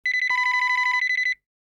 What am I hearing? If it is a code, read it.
There is no code.